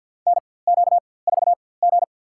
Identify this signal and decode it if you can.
Morse code, heard as a keyed tone, about 53 wpm, spaced as for 24 wpm, 700 Hz. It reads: NQ4G
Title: USA Call Signs - 4x Character Spacing - 50wpm